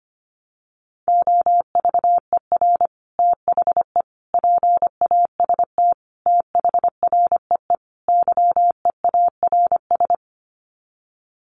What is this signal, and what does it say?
Morse code, heard as a keyed tone, about 25 wpm, 700 Hz. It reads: OVER THE PAST THREE YEARS
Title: Sets of 5 Words Encore2 25wpm